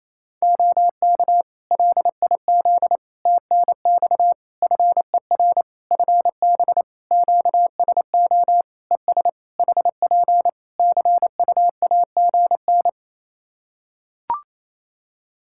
Morse code, heard as a keyed tone, about 28 wpm, 700 Hz. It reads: OK LIZ TNX FER FB QSO ES HP CUAGN
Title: K1USN SST QSO Practice 28wpm